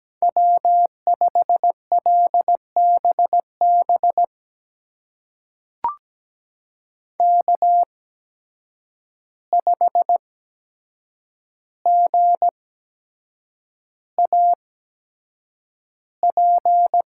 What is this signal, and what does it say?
Morse code, heard as a keyed tone, about 17 wpm, spaced as for 7 wpm, 700 Hz. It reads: W5LBB K5GAP
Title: USA Call Signs - 5 Characters - 8x Character Spacing - 17wpm